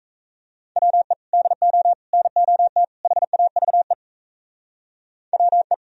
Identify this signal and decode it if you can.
Morse code, heard as a keyed tone, about 35 wpm, 700 Hz. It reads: WE DO NOT HAVE WE